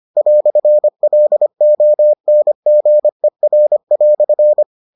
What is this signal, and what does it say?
Morse code, heard as a keyed tone, about 25 wpm, 600 Hz. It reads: "LONGER"